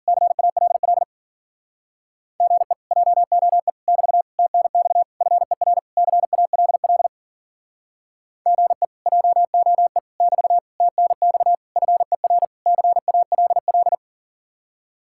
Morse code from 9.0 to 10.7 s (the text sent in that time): JOE <BT>